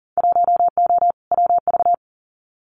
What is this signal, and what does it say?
Morse code, heard as a keyed tone, about 38 wpm, 700 Hz. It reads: 1O WV
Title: Winter Field Day Exchanges 40wpm